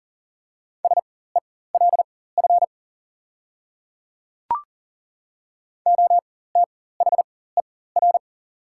Morse code, heard as a keyed tone, about 40 wpm, spaced as for 19 wpm, 700 Hz. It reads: SELF OTHER
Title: 4x Character Spacing - Top 200 Words - 40wpm